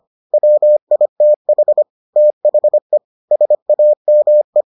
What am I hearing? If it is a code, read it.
Morse code, heard as a keyed tone, about 25 wpm, 600 Hz. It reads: WITH THE SAME